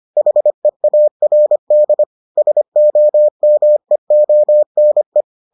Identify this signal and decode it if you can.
Morse code, heard as a keyed tone, about 25 wpm, 600 Hz. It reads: HEARD SOMEONE